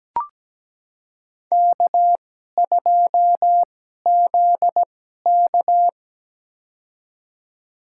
Morse code, heard as a keyed tone, about 17 wpm, spaced as for 12 wpm, 700 Hz. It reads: K2ZK
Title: USA Call Signs - 4 Characters - 2x Character Spacing - 17wpm